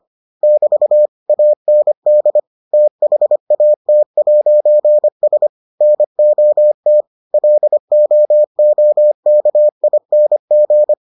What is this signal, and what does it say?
Morse code, heard as a keyed tone, about 25 wpm, 600 Hz. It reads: <BT> AND THAT'S NOT LOOKING